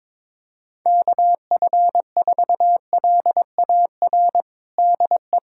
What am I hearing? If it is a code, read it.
Morse code, heard as a keyed tone, about 22 wpm, 700 Hz. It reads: KF4LAR DE
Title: USA Callsign DE Callsign - 6 Characters - 22wpm